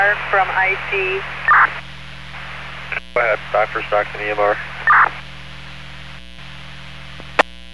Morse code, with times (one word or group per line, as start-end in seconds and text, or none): none